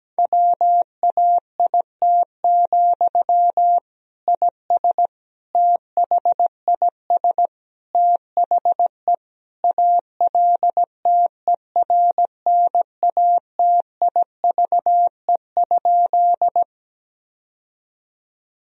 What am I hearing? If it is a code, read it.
Morse code, heard as a keyed tone, about 17 wpm, 700 Hz. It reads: WAIT, IS THIS THE ALTERNATIVE?
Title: Sentences from Top 1000 Words - Review Part 4 - 17wpm